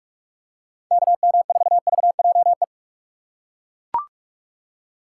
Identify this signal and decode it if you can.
Morse code, heard as a keyed tone, about 42 wpm, 700 Hz. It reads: KM4VJE
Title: USA Call Signs - 6 Characters - 4x Character Spacing - 45wpm